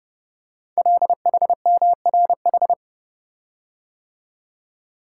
Morse code, heard as a keyed tone, about 28 wpm, 700 Hz. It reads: LHMRH
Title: Code Groups - 5 Letters - 30wpm